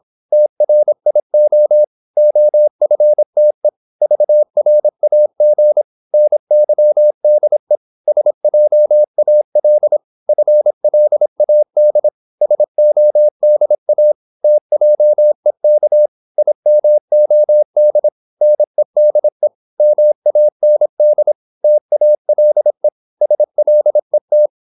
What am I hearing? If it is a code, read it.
Morse code, heard as a keyed tone, about 26 wpm, 600 Hz. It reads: TRIO OFTE VRAG NYDE SJAL FLAD SODA TJEK IMOD NEDE MAND TALE SLET
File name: Korte ord DK 26wpm